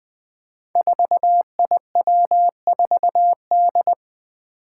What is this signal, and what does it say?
Morse code, heard as a keyed tone, about 20 wpm, 700 Hz. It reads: 4IW4D
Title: Five Challenging Letter-Number Combinations 20wpm